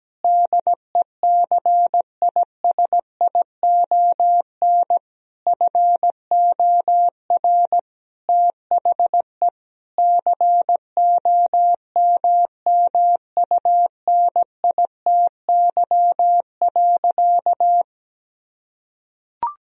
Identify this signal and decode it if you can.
Morse code, heard as a keyed tone, about 17 wpm, 700 Hz. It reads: DECISION FOR THE COMMUNITY.
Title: Sentences from Top 1000 Words - Review Part 4 - 17wpm